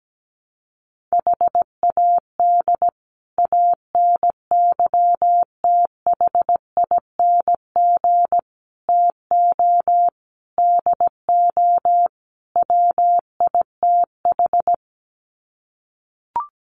Morse code, heard as a keyed tone, about 17 wpm, 700 Hz. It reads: HAD ANYTHING TO DO WITH
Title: Sets of 5 Words Encore 17wpm